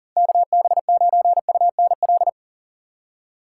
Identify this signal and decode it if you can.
Morse code, heard as a keyed tone, about 40 wpm, 700 Hz. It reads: KB9UNL
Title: USA Call Signs - 6 Characters - 6x Character Spacing - 40wpm